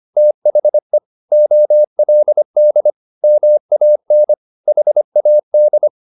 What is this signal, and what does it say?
Morse code, heard as a keyed tone, about 25 wpm, 600 Hz. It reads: THE OLD MAN HAD